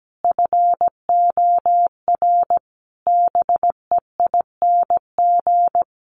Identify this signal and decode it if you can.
Morse code, heard as a keyed tone, about 17 wpm, 700 Hz. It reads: FOR BEING
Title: Sets of 5 Words Encore 17wpm